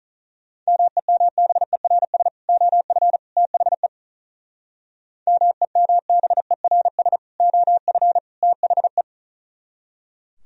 Morse code, heard as a keyed tone, about 45 wpm, 700 Hz. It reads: MEMBERS OF THE MEMBERS OF THE